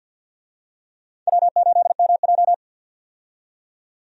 Morse code, heard as a keyed tone, about 52 wpm, 700 Hz. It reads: W8MJ
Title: USA Call Signs - 4 Characters - 50wpm